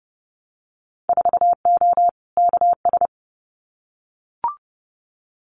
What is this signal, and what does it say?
Morse code, heard as a keyed tone, about 30 wpm, 700 Hz. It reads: 4O KS
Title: Winter Field Day Exchanges Encore 30wpm